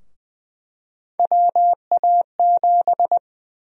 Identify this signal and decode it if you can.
Morse code, heard as a keyed tone, about 20 wpm, 700 Hz. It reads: WA7
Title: USA Call Sign Prefixes - Repeated 2x - 20wpm